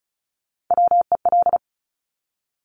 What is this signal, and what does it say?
Morse code, heard as a keyed tone, about 36 wpm, 700 Hz. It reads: WEL
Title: Top Three Letter Combinations in Words 35wpm